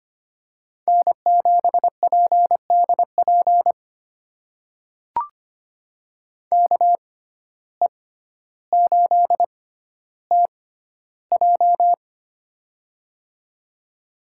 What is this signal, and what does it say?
Morse code, heard as a keyed tone, about 25 wpm, spaced as for 13 wpm, 700 Hz. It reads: N7PDP KE8TJ
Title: USA Call Signs - 5 Characters - 6x Character Spacing - 25wpm